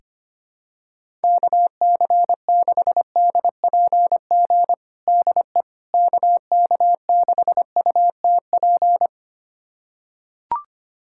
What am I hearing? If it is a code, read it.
Morse code, heard as a keyed tone, about 25 wpm, 700 Hz. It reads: KC6DPG DE KK6UTP
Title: USA Callsign DE Callsign - 6 Characters - 25wpm